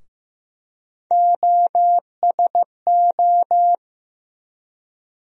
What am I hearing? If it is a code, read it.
Morse code, heard as a keyed tone, about 15 wpm, 700 Hz. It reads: OSO
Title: USA Call Sign Postfixes - 3 Letters 15wpm